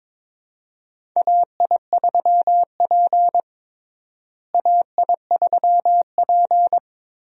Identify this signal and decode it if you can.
Morse code, heard as a keyed tone, about 22 wpm, 700 Hz. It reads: AI3P AI3P
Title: USA Call Signs - 4 Characters - Repeated 3x - 22wpm